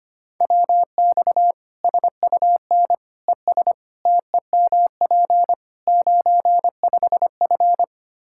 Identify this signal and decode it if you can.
Morse code, heard as a keyed tone, about 25 wpm, 700 Hz. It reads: WX SUN ES TEMP 95F
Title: K1USN SST QSO Practice 25wpm